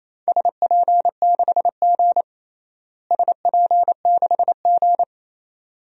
Morse code, heard as a keyed tone, about 28 wpm, 700 Hz. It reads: SP6G SP6G
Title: DXCC-100 Call Signs - 4 Characters - Repeated 2x - 28wpm